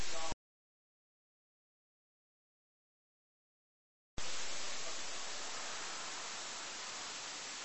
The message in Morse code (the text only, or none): none